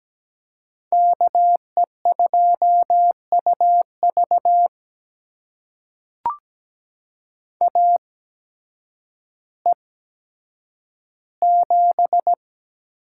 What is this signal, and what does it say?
Morse code, heard as a keyed tone, about 17 wpm, spaced as for 9 wpm, 700 Hz. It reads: KE2UV AE7
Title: USA Call Signs - 5 Characters - 8x Character Spacing - 17wpm